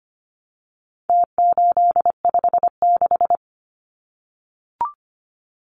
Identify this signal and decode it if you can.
Morse code, heard as a keyed tone, about 25 wpm, 700 Hz. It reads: T856
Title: Numbers - Contest Style - 4-Digit 25wpm